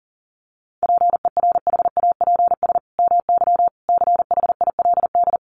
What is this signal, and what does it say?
Morse code, heard as a keyed tone, about 40 wpm, 700 Hz. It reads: PERHAPS MY CHILD
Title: Sentences from Top 1100 Words 40wpm